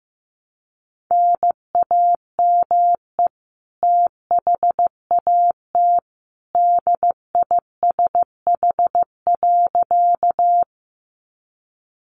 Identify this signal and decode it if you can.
Morse code, heard as a keyed tone, about 15 wpm, 700 Hz. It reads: NAME THAT DISH.
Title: Sentences from Top 1300 Words 15wpm